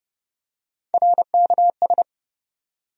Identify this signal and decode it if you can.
Morse code, heard as a keyed tone, about 30 wpm, 700 Hz. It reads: RKS
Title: USA Call Sign Postfixes - 3 Letters 30wpm